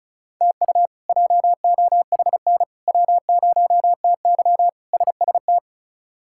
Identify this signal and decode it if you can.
Morse code, heard as a keyed tone, about 36 wpm, 700 Hz. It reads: TU JOHN W0TY SST
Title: K1USN SST QSO Practice - 35wpm